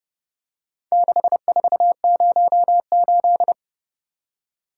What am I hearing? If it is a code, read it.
Morse code, heard as a keyed tone, about 31 wpm, 700 Hz. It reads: 6408